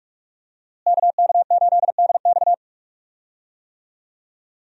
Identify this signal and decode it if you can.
Morse code, heard as a keyed tone, about 46 wpm, 700 Hz. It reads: KK8DX
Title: USA Call Signs - 5 Characters - Repeated 2x - 45wpm